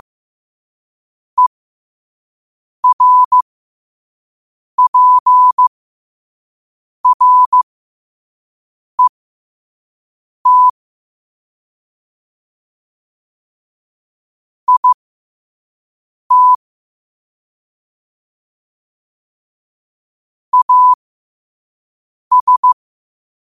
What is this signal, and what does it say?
Morse code, heard as a keyed tone, about 15 wpm, spaced as for 5 wpm, 1000 Hz. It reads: ERPRET IT AS
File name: qotdfiles / 5wpm